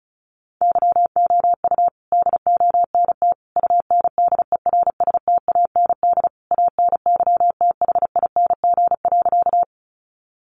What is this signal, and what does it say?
Morse code, heard as a keyed tone, about 34 wpm, 700 Hz. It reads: YOU DONT UNDERSTAND ANYTHING.
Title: Sentences from Top 1000 Words 35wpm